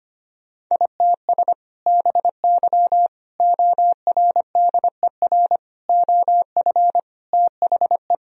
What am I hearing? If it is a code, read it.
Morse code, heard as a keyed tone, about 25 wpm, 700 Hz. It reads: ITS BY ORDER OF THE
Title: Sentences from Top 1000 Words - Review Part 3 - 25wpm